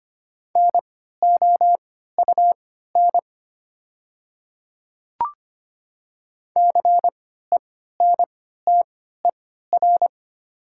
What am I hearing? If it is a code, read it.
Morse code, heard as a keyed tone, about 25 wpm, spaced as for 14 wpm, 700 Hz. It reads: NOUN CENTER
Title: 3x Character Spacing - Top 500 Words - 25wpm